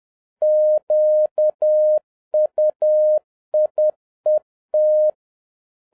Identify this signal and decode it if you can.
Morse code, heard as a keyed tone, about 10 wpm, 610 Hz. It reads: QUIET
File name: 2011-07-mp3 / n46 JOHN K3TN 10wpm